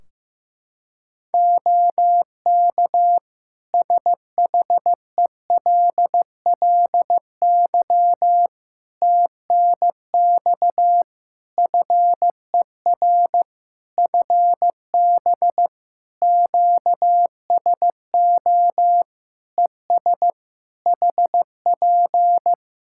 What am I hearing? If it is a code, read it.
Morse code, heard as a keyed tone, about 15 wpm, 700 Hz. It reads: OK SHELLY TNX FER FB QSO ES HP
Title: Practice QSO - Exchange 4 - Ending - 15wpm